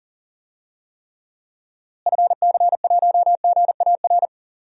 Morse code, heard as a keyed tone, about 40 wpm, 700 Hz. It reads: FC1GAR
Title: DXCC-100 Call Signs - 6 Characters - Repeated 3x - 40wpm